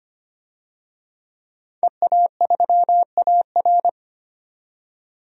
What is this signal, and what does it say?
Morse code, heard as a keyed tone, about 25 wpm, 700 Hz. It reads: EA3AR
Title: DXCC-100 Call Signs - 5 Characters - 25wpm